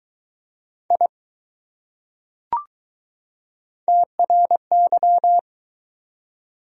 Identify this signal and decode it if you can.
Morse code, heard as a keyed tone, about 20 wpm, 700 Hz. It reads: I TRY